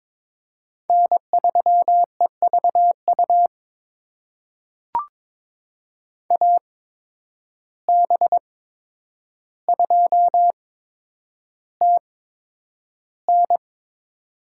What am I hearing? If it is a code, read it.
Morse code, heard as a keyed tone, about 21 wpm, spaced as for 9 wpm, 700 Hz. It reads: N3EVU AB2TN
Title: USA Call Signs - 5 Characters - 8x Character Spacing - 22wpm